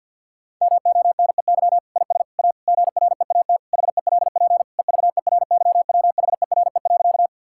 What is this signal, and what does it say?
Morse code, heard as a keyed tone, about 50 wpm, 700 Hz. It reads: MONEY IS A GREAT HELP EVERYWHERE.
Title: Sentences from Top 1600 Words 50wpm